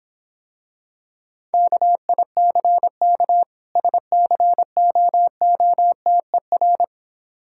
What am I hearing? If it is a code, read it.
Morse code, heard as a keyed tone, about 26 wpm, 700 Hz. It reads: KICK SCOOTER